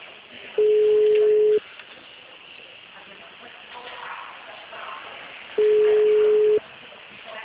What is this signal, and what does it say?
Ringback tone (Europe, 425 Hz).
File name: out- -2013- - - .1939